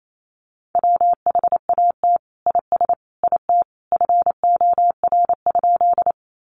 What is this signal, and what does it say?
Morse code, heard as a keyed tone, about 28 wpm, 700 Hz. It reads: WHAT IS IT FOR?
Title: Sentences from Top 100 Words 28wpm